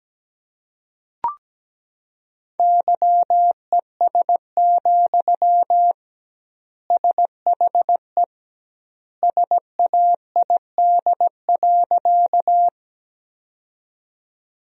Morse code, heard as a keyed tone, about 17 wpm, 700 Hz. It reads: YES, SHE SAID.
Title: 2x Word-Spacing - Sentences from Top 500 Words - 17wpm